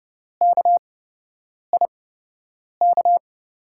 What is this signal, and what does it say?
Morse code, heard as a keyed tone, about 30 wpm, spaced as for 8 wpm, 700 Hz. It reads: KIK
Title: USA Call Signs - 8x Character Spacing - 30wpm